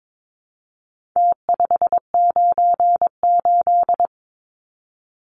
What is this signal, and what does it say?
Morse code, heard as a keyed tone, about 22 wpm, 700 Hz. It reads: T598
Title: Numbers - Contest Style - 4-Digit 22wpm